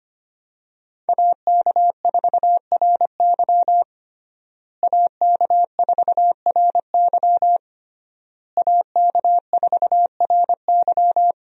Morse code, heard as a keyed tone, about 25 wpm, 700 Hz. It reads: AK4RY AK4RY AK4RY
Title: USA Call Signs - 5 Characters - Repeated 3x - 25wpm